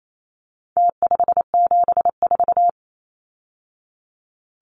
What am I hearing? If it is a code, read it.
Morse code, heard as a keyed tone, about 28 wpm, 700 Hz. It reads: T574
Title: Numbers - Contest Style - 4-Digit 28wpm